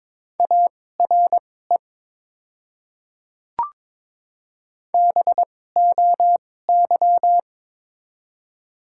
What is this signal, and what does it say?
Morse code, heard as a keyed tone, about 22 wpm, spaced as for 16 wpm, 700 Hz. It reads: ARE BOY
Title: 2x Character Spacing - Top 200 Words - 22wpm